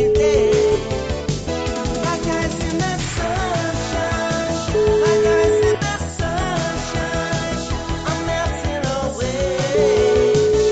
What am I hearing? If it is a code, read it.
Ringback tone (Europe, 425 Hz), with noise.